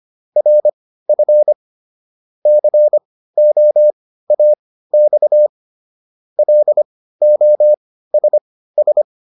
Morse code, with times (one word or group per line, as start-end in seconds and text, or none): 0.4-1.5 RF
2.5-5.5 COAX
6.4-9.0 LOSS